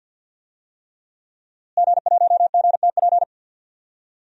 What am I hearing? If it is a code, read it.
Morse code, heard as a keyed tone, about 51 wpm, 700 Hz. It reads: G1GTP
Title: DXCC-100 Call Signs - 5 Characters - Repeated 3x - 50wpm